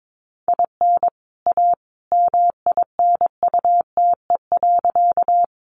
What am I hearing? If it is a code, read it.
Morse code, heard as a keyed tone, about 22 wpm, 700 Hz. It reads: IN A MINUTE.
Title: Sentences from Top 500 Words 22wpm